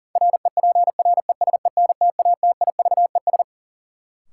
Morse code, heard as a keyed tone, about 40 wpm, 700 Hz. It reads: REPRESENTATIVES
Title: Sets of 5 Words Encore - New Word Drill - 40wpm